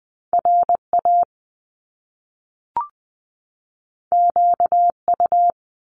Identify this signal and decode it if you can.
Morse code, heard as a keyed tone, about 20 wpm, 700 Hz. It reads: R A Q U
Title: Top Two Letter Combinations in Words 20wpm